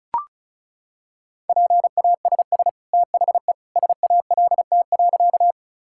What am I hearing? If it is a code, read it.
Morse code, heard as a keyed tone, about 36 wpm, 700 Hz. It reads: PASS THE SALT.
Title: Sentences from Top 1000 Words - Review Part 5 - 35wpm